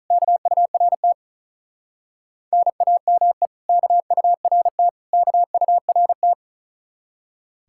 Morse code, heard as a keyed tone, about 44 wpm, 700 Hz. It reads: KURT NAME KURT KURT